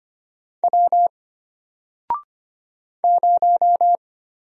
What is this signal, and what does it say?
Morse code, heard as a keyed tone, about 25 wpm, 700 Hz. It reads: W 0
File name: Single-Letter-Number-25wpm